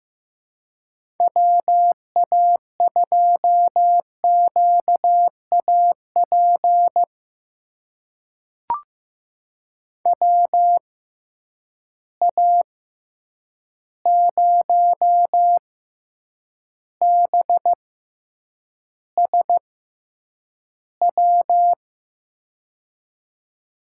Morse code, heard as a keyed tone, about 15 wpm, spaced as for 8 wpm, 700 Hz. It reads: WA2QAP WA0BSW
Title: USA Call Signs - 6 Characters - 6x Character Spacing - 15wpm